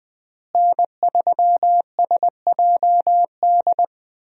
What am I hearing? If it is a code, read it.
Morse code, heard as a keyed tone, about 20 wpm, 700 Hz. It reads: N3SJD
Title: USA Call Signs - 5 Characters - Repeated 3x - 20wpm